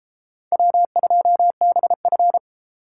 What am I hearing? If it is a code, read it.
Morse code, heard as a keyed tone, about 32 wpm, 700 Hz. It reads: W2BF